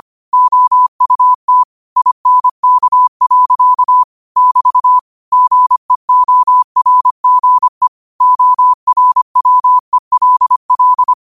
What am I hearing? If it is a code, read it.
Morse code, heard as a keyed tone, about 25 wpm, 1000 Hz. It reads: OUT INK. <BT> GEORGE ORWELL
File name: qotdfiles / 25wpm